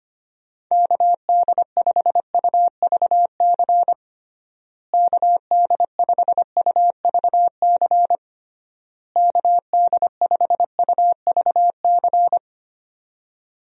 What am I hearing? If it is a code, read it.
Morse code, heard as a keyed tone, about 25 wpm, 700 Hz. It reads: KD5UVC KD5UVC KD5UVC
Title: USA Call Signs - 6 Characters - Repeated 3x - 25wpm